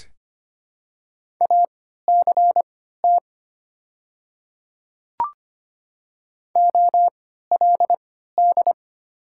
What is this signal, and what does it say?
Morse code, heard as a keyed tone, about 25 wpm, spaced as for 14 wpm, 700 Hz. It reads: ACT OLD
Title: 3x Character Spacing - Top 200 Words - 25wpm